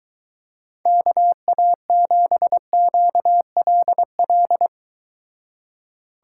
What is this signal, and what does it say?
Morse code, heard as a keyed tone, about 23 wpm, 700 Hz. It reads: KA7QLL